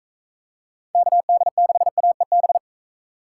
Morse code, heard as a keyed tone, about 41 wpm, 700 Hz. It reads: KD6AEB